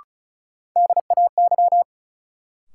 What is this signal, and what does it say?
Morse code, heard as a keyed tone, about 36 wpm, 700 Hz. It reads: DAY
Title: Sentences from Top 200 Words - New Word Drill - 35wpm